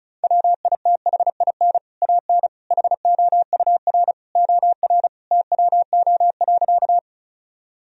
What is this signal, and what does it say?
Morse code, heard as a keyed tone, about 36 wpm, 700 Hz. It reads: WITHIN AN HOUR OR TWO.
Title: Sentences from Top 1000 Words - Review Part 4 - 35wpm